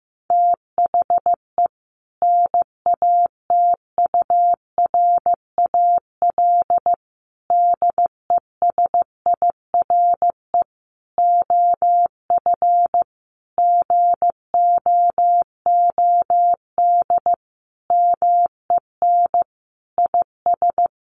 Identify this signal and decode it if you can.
Morse code, heard as a keyed tone, about 14 wpm, 700 Hz. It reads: THE NATURAL DESIRE OF GOOD MEN IS
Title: Sentences from Top 1100 Words 15wpm